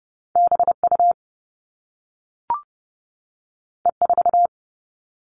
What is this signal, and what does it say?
Morse code, heard as a keyed tone, about 30 wpm, 700 Hz. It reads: BU E4
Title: All Cut Numbers - 2-Digit 30wpm